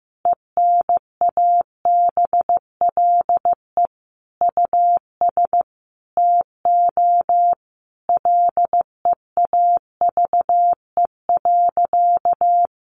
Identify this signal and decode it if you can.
Morse code, heard as a keyed tone, about 15 wpm, 700 Hz. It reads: ENABLE US TO LEAVE.
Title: Sentences from Top 1500 Words 15wpm